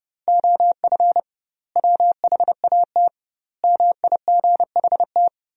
Morse code, heard as a keyed tone, about 30 wpm, 700 Hz. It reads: OF WHAT MIGHT
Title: 2x Word-Spacing - Sentences from Top 400 Words - 30wpm